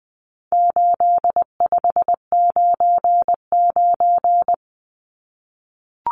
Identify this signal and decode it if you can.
Morse code, heard as a keyed tone, about 20 wpm, 700 Hz. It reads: 8599
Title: Numbers - 4-Digit 20wpm